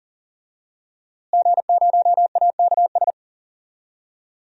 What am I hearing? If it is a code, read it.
Morse code, heard as a keyed tone, about 40 wpm, 700 Hz. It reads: G0AKS
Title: DXCC-100 Call Signs - 5 Characters - Repeated 2x - 40wpm